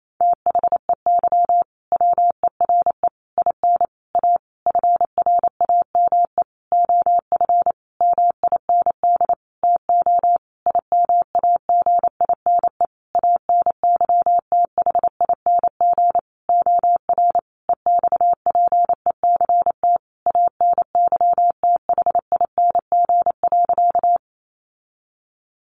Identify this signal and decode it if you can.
Morse code, heard as a keyed tone, about 28 wpm, 700 Hz. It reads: THEY WERE IN A FRAME OF MIND TO IMAGINE ANYTHING OR EXPECT ANYTHING.
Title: Sentences from Top 1400 Words 28wpm